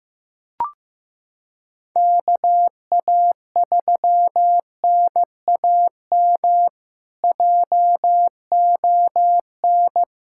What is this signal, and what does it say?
Morse code, heard as a keyed tone, about 15 wpm, 700 Hz. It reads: KA3NAM JON
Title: K1USN SST QSO Practice - 15wpm